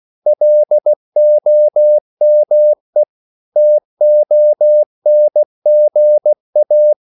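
Morse code, heard as a keyed tone, about 16 wpm, 600 Hz. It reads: LOME TONGA
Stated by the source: CW practice of national capitals at 16 words per minute, from ON6ZQ.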